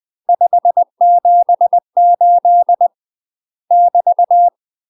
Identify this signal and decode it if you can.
Morse code, heard as a keyed tone, about 20 wpm, 700 Hz. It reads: 578 <BT>
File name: UK MASTER QSOs TRAINING / 20WPM / UK MASTER QSOs 0012